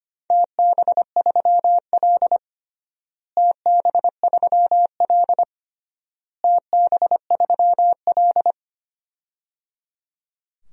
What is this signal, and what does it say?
Morse code, heard as a keyed tone, about 25 wpm, 700 Hz. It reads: TB3L TB3L TB3L
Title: DXCC-100 Call Signs - 4 Characters - Repeated 3x - 25wpm